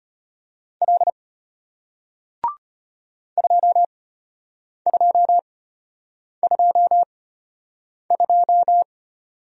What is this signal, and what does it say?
Morse code, heard as a keyed tone, about 33 wpm, spaced as for 8 wpm, 690 Hz. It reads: L 2222